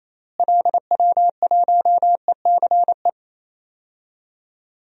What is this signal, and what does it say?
Morse code, heard as a keyed tone, about 28 wpm, 700 Hz. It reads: LW1ECE
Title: DXCC-100 Call Signs - 6 Characters - Repeated 3x - 28wpm